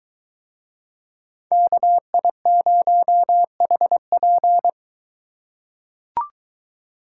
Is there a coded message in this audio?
Yes: Morse code from 1.5 to 4.7 s, heard as a keyed tone, about 22 wpm, 700 Hz. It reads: KI0HP